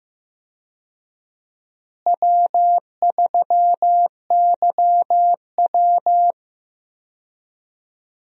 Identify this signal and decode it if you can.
Morse code, heard as a keyed tone, about 15 wpm, 700 Hz. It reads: W3YW
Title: USA Call Signs - 4 Characters - Repeated 3x - 15wpm